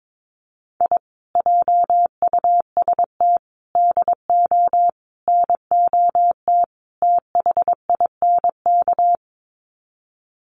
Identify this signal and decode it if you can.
Morse code, heard as a keyed tone, about 22 wpm, 700 Hz. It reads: I JUST DO NOT THINK
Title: Sets of 5 Words Encore 22wpm